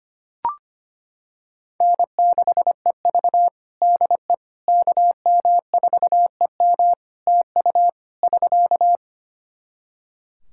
Morse code, heard as a keyed tone, about 25 wpm, 700 Hz. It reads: N6EV DE KM4EM TU <SK>
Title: Practice QSO - Exchange 4 - Ending - 25wpm